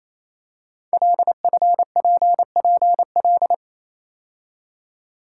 Morse code, heard as a keyed tone, about 28 wpm, 700 Hz. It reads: LFPPL
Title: Code Groups - 5 Letters - 28wpm